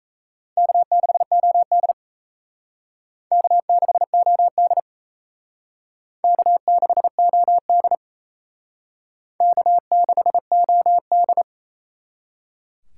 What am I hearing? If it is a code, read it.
Morse code, heard as a keyed tone, about 34 wpm, 700 Hz. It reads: K6OD K6OD K6OD K6OD
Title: Speed Racing - USA 1x2 Call Signs 28wpm